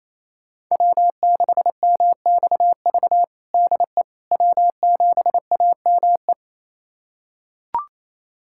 Morse code, heard as a keyed tone, about 27 wpm, 700 Hz. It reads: W6MXV DE W7AME
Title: USA Callsign DE Callsign - 5 Characters - 28wpm